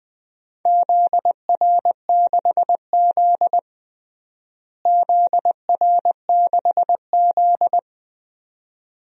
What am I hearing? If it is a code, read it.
Morse code, heard as a keyed tone, about 20 wpm, 700 Hz. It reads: ZR6Z ZR6Z
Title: DXCC-100 Call Signs - 4 Characters - Repeated 2x - 20wpm